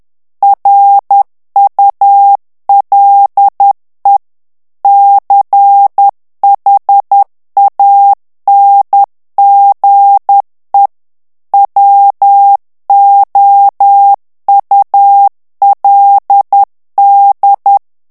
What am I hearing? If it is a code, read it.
Morse code, heard as a keyed tone, about 10 wpm, 800 Hz. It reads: RULE CHANGE WOULD
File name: ANGLAIS texte clair 3 vitesse 10 mots minute